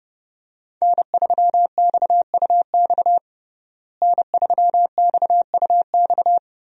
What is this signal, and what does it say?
Morse code, heard as a keyed tone, about 30 wpm, 700 Hz. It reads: N3XUX N3XUX
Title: USA Call Signs - 5 Characters - Repeated 2x - 30wpm